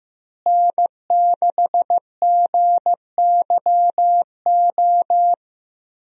Morse code, heard as a keyed tone, about 15 wpm, 700 Hz. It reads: N6GYO
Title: USA Call Signs - 5 Characters - Repeated 3x - 15wpm